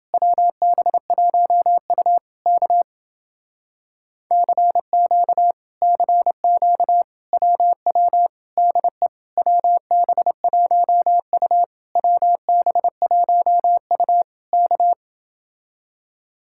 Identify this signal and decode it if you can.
Morse code, heard as a keyed tone, about 29 wpm, 700 Hz. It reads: WB1U K CQ CQ WW DE WB1U WB1U K